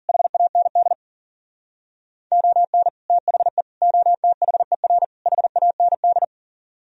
Morse code, heard as a keyed tone, about 52 wpm, 700 Hz. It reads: HAND ON THE OTHER HAND